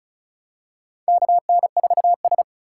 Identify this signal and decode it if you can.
Morse code, heard as a keyed tone, about 36 wpm, 700 Hz. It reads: KN4S
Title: USA Call Signs - 4 Characters - 3x Character Spacing - 35wpm